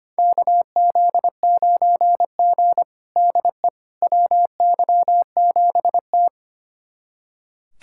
Morse code, heard as a keyed tone, about 25 wpm, 700 Hz. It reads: KZ9G DE WY7T
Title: USA Callsign DE Callsign - 4 Characters - 25wpm